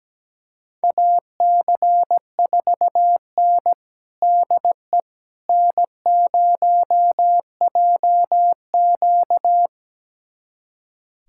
Morse code, heard as a keyed tone, about 17 wpm, 700 Hz. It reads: AC4N DE N0JQ
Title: USA Callsign DE Callsign - 4 Characters - 17wpm